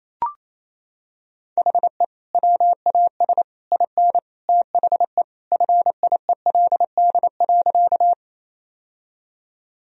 Morse code, heard as a keyed tone, about 28 wpm, 700 Hz. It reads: HE WAS IN THE FIELD.
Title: Sentences from Top 1000 Words - Review Part 2 - 28wpm